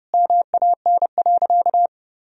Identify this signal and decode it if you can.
Morse code, heard as a keyed tone, about 32 wpm, 700 Hz. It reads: MAN.
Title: Sentences from Top 300 Words and Abbreviations 30wpm